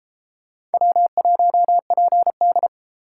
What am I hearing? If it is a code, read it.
Morse code, heard as a keyed tone, about 30 wpm, 700 Hz. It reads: W1PD